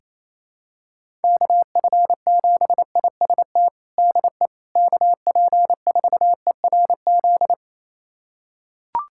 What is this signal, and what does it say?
Morse code, heard as a keyed tone, about 28 wpm, 700 Hz. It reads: KF7IST DE KP4ERZ
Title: USA Callsign DE Callsign - 6 Characters - 28wpm